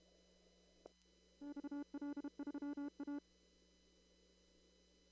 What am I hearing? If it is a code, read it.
Morse code, heard as a keyed tone, about 32 wpm, 290 Hz. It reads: XL3A
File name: CW / kt5j wpx cw-op-k5pi